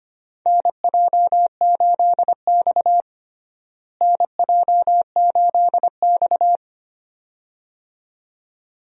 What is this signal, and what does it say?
Morse code, heard as a keyed tone, about 25 wpm, 700 Hz. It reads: NJ8X NJ8X
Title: USA Call Signs - 4 Characters - Repeated 2x - 25wpm